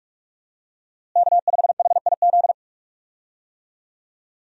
Morse code, heard as a keyed tone, about 41 wpm, 700 Hz. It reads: K5HIZ
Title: USA Call Signs - 5 Characters - Repeated 2x - 45wpm